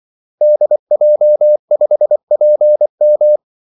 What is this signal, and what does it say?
Morse code, heard as a keyed tone, about 24 wpm, 600 Hz. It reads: DJ5PM
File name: Callsigns-24wpm